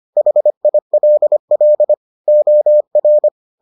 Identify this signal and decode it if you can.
Morse code, heard as a keyed tone, about 25 wpm, 600 Hz. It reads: HILL OR